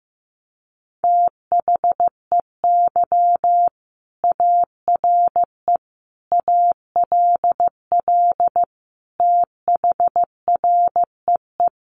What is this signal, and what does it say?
Morse code, heard as a keyed tone, about 15 wpm, 700 Hz. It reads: THEY ARE ALL THREE
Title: Sentences from Top 1500 Words 15wpm